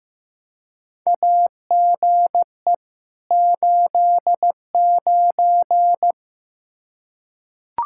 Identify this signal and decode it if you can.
Morse code, heard as a keyed tone, about 15 wpm, 700 Hz. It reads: AGE 89
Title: K1USN SST QSO Practice 15wpm